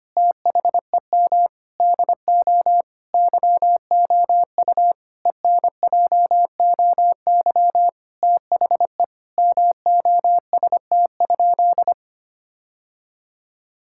Morse code, heard as a keyed tone, about 25 wpm, 700 Hz. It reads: THEM DO YOU ENJOY THE MOST?
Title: Sentences from Top 1000 Words - Review Part 5 - 25wpm